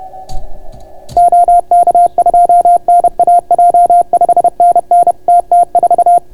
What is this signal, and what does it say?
Morse code, heard as a keyed tone, about 31 wpm, 690 Hz. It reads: OK2NAJ 5NN TT4